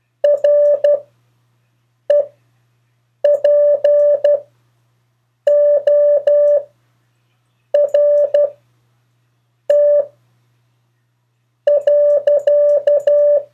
Morse code, heard as a keyed tone, about 12 wpm, spaced as for 6 wpm, 580 Hz. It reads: REPORT.
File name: 4of4-sentences-mistakes-test